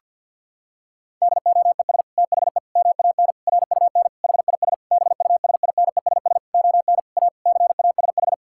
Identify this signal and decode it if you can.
Morse code, heard as a keyed tone, about 51 wpm, 700 Hz. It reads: DOES THE MAN RUN HIS BUSINESS ON A CASH
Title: Sentences from Top 1000 Words - Review Part 5 - 50wpm